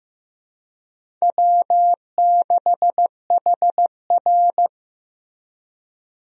Morse code, heard as a keyed tone, about 14 wpm, 700 Hz. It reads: W6HR
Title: USA Call Signs - 4 Characters - Repeated 3x - 15wpm